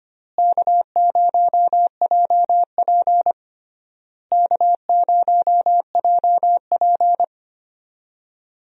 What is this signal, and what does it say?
Morse code, heard as a keyed tone, about 25 wpm, 700 Hz. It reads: K0JP K0JP
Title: USA Call Signs - 4 Characters - Repeated 3x - 25wpm